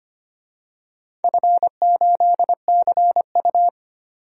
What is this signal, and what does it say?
Morse code, heard as a keyed tone, about 25 wpm, 700 Hz. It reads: F8CU
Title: DXCC-100 Call Signs - 4 Characters - 25wpm